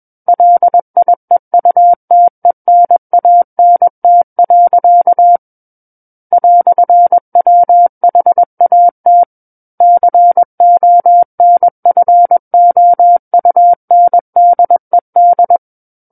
Morse code, heard as a keyed tone, about 21 wpm, 700 Hz. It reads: LIEUTENANT. "WHAT CONFOUNDED
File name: Book One / 12-Chapter 12 - 21 WPM